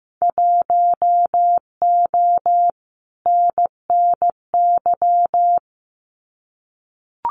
Morse code, heard as a keyed tone, about 15 wpm, 700 Hz. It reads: 1O NNY
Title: Winter Field Day Exchanges 15wpm